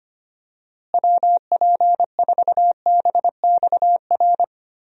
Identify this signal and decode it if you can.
Morse code, heard as a keyed tone, about 25 wpm, 700 Hz. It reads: WP4BXR
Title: USA Call Signs - 6 Characters - Repeated 2x - 25wpm